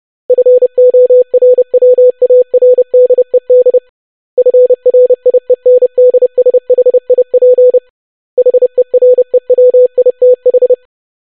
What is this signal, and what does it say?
Morse code, heard as a keyed tone, about 30 wpm, 500 Hz. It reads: FORWARDED FRIENDSHIP HEREWITH
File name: longwords-25wpm